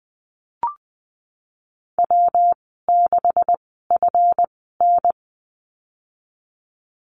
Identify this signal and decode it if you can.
Morse code, heard as a keyed tone, about 20 wpm, spaced as for 15 wpm, 700 Hz. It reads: W6FN
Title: 154 Lesson 25g - Letter p - Call Signs 20wpm